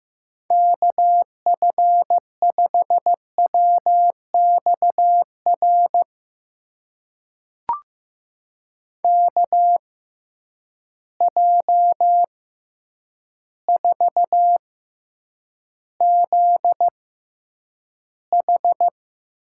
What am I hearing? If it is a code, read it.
Morse code, heard as a keyed tone, about 15 wpm, spaced as for 8 wpm, 700 Hz. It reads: KF5WXR KJ4ZH
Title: USA Call Signs - 6 Characters - 6x Character Spacing - 15wpm